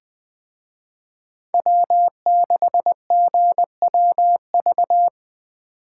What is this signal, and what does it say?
Morse code, heard as a keyed tone, about 23 wpm, 700 Hz. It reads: W6GWV
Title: Speed Racing - USA 1x3 Call Signs 20wpm